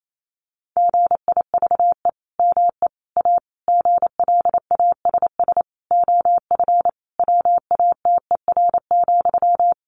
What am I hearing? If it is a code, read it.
Morse code, heard as a keyed tone, about 28 wpm, 700 Hz. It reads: GIVE ME A GLASS OF WATER,
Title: Sentences from Top 1500 Words 28wpm